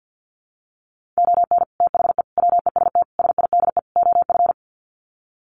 Morse code, heard as a keyed tone, about 50 wpm, 700 Hz. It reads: ON THE WEST SIDE OF
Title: Sets of 5 Words Encore2 50wpm